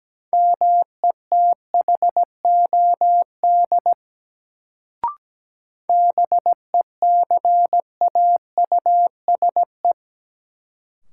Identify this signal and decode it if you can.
Morse code, heard as a keyed tone, about 17 wpm, 700 Hz. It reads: METHOD BECAUSE
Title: Sentences from Top 600 Words - New Word Drill - 17wpm